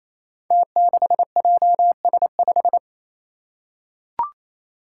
Morse code, heard as a keyed tone, about 28 wpm, 700 Hz. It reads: T6JS5
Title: Five Challenging Letter-Number Combinations 28wpm